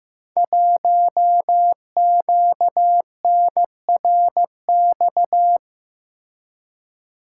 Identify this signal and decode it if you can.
Morse code, heard as a keyed tone, about 15 wpm, 700 Hz. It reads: 1QNRX
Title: Code Groups - 5 Letter:Numbers - 15wpm